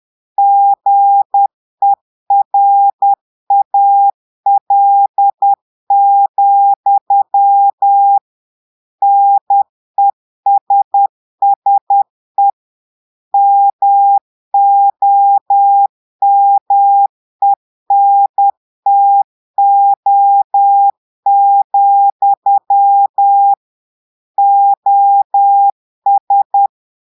Morse code, heard as a keyed tone, about 10 wpm, 800 Hz. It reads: GERAL, NESSE MOMENTO, OS
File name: portugais texte en clair 2 vitesse 10 mots minutes